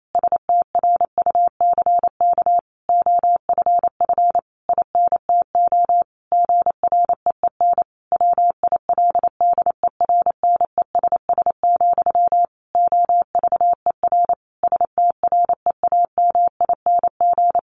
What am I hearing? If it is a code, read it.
Morse code, heard as a keyed tone, about 28 wpm, 700 Hz. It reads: STRUCK OFF INTO GREEN WILDERNESS, OVER STREAMING
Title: A Sound Of Thunder 28wpm